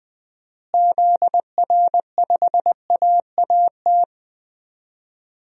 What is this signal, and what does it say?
Morse code, heard as a keyed tone, about 20 wpm, 700 Hz. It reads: ZR5AAT
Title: DXCC-100 Call Signs - 6 Characters - Repeated 2x - 20wpm